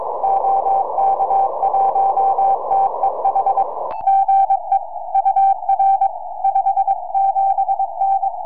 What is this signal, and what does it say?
Morse code, heard as a keyed tone, about 22 wpm, 780 Hz. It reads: OK1TEH GE UR 57N
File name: oy3je 4m